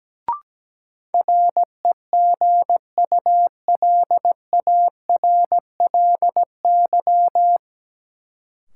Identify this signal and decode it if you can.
Morse code, heard as a keyed tone, about 17 wpm, 700 Hz. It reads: REGULARLY
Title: Sentences from Top 1400 Words - New Word Drill - 17wpm